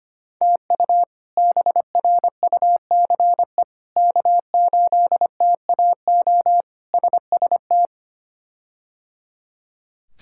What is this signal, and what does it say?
Morse code, heard as a keyed tone, about 25 wpm, 700 Hz. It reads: TU BRUCE K8TAO SST
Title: K1USN SST QSO Practice - 25wpm